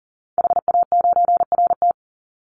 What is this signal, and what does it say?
Morse code, heard as a keyed tone, about 40 wpm, 700 Hz. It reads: HA9RT
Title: Call Signs - 40wpm